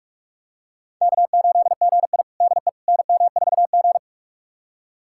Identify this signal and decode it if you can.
Morse code, heard as a keyed tone, about 41 wpm, 700 Hz. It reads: K8GI DE NM4G
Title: USA Callsign DE Callsign - 4 Characters - 45wpm